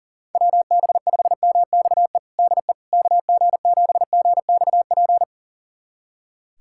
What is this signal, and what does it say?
Morse code, heard as a keyed tone, about 37 wpm, 700 Hz. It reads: WB5MXE DE KG7GXP
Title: USA Callsign DE Callsign - 6 Characters - 40wpm